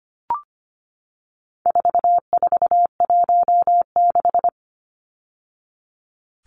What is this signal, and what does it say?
Morse code, heard as a keyed tone, about 25 wpm, 700 Hz. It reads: 4416
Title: Numbers - 4-Digit 25wpm